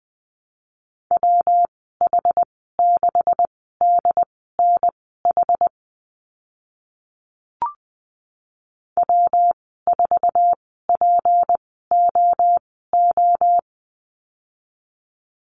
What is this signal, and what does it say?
Morse code, heard as a keyed tone, about 20 wpm, spaced as for 14 wpm, 700 Hz. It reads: WH6DNH W4POO
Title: 154 Lesson 25g - Letter p - Call Signs 20wpm